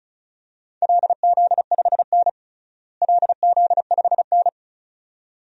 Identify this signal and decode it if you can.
Morse code, heard as a keyed tone, about 36 wpm, 700 Hz. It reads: LZ5N LZ5N
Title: DXCC-100 Call Signs - 4 Characters - Repeated 2x - 35wpm